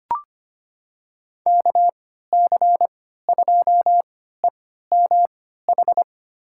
Morse code, heard as a keyed tone, about 25 wpm, spaced as for 14 wpm, 700 Hz. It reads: KC2EMH
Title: USA Call Signs - 6 Characters - 3x Character Spacing - 25wpm